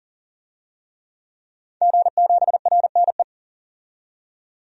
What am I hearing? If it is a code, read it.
Morse code, heard as a keyed tone, about 40 wpm, 700 Hz. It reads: G7RNE
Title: DXCC-100 Call Signs - 5 Characters - 40wpm